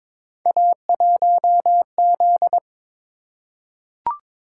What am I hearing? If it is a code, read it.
Morse code, heard as a keyed tone, about 22 wpm, 700 Hz. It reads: A1Z
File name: practice / 22wpm / Three Letter-Number Combinations - 22wpm